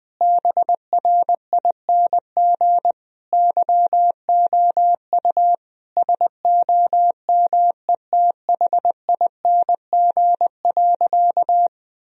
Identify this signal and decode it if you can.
Morse code, heard as a keyed tone, about 20 wpm, 700 Hz. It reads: BRING YOU SOMETHING.
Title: Sentences from Top 1000 Words - Review Part 3 - 20wpm